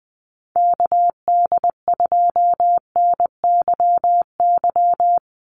Morse code, heard as a keyed tone, about 20 wpm, 700 Hz. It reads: KD2NYY
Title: USA Call Signs 20wpm